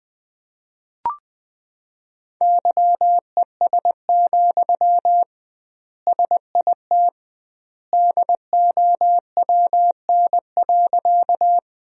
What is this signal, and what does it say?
Morse code, heard as a keyed tone, about 20 wpm, 700 Hz. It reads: YES, SIT DOWN.
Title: 2x Word-Spacing - Sentences from Top 500 Words - 20wpm